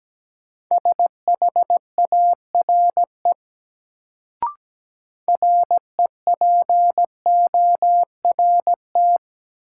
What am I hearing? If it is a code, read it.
Morse code, heard as a keyed tone, about 17 wpm, 700 Hz. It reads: SHARE REPORT
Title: Sentences from Top 900 Words - New Word Drill - 17wpm